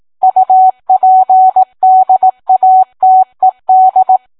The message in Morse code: UPDATED